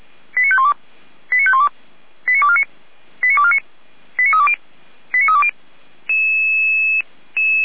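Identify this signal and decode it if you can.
Five-tone selective call (ZVEI), heard as quick runs of five tones, tones of about 70 ms. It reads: 89512 89512 89249 89249 89320 89320
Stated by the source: ZVEI five-tone selective call received on an SDR.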